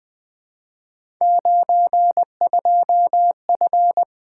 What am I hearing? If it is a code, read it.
Morse code, heard as a keyed tone, about 20 wpm, 700 Hz. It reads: 92F
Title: Three Letter-Number Combinations - 20wpm